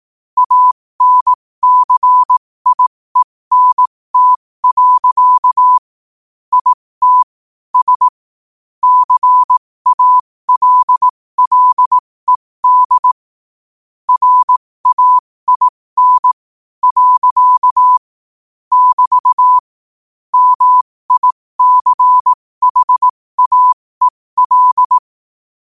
Morse code, heard as a keyed tone, about 18 wpm, spaced as for 15 wpm, 1000 Hz. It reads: ANCIENT. IT S CALLED RAIN. <BT> MICHAEL
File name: qotdfiles / 15wpm